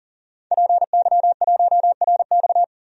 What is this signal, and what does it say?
Morse code, heard as a keyed tone, about 39 wpm, 700 Hz. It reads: PY1RX
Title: DXCC-100 Call Signs - 5 Characters - 40wpm